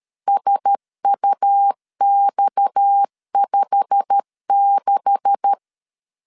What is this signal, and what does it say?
Morse code, heard as a keyed tone, about 13 wpm, 800 Hz. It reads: SUX56
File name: LEZIONI / 10 SIGNS / 10A SLASH 12wpm 2314